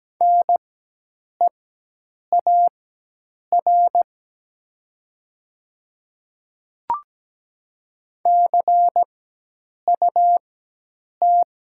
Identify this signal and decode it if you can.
Morse code, heard as a keyed tone, about 17 wpm, spaced as for 8 wpm, 700 Hz. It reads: NEAR CUT
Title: 4x Character Spacing - Top 300 Words - 17wpm